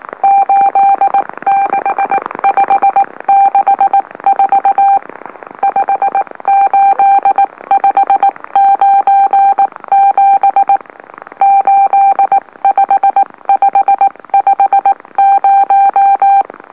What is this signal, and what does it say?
Morse code, heard as a keyed tone, about 18 wpm, 790 Hz. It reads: 86564 58597 85550
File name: M04- -2345utc-LOLO